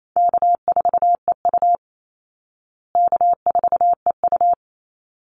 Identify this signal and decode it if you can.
Morse code, heard as a keyed tone, about 28 wpm, 700 Hz. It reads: K4EU K4EU
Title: Call Signs Repeated 3x - 28wpm